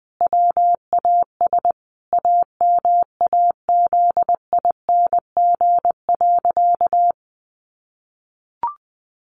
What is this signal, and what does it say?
Morse code, heard as a keyed tone, about 20 wpm, 700 Hz. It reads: WAS AMAZING.
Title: Sentences from Top 1500 Words 20wpm